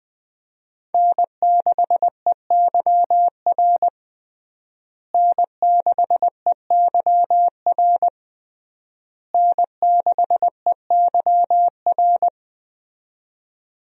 Morse code, heard as a keyed tone, about 21 wpm, 700 Hz. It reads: N6EYR N6EYR N6EYR
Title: USA Call Signs - 5 Characters - Repeated 3x - 20wpm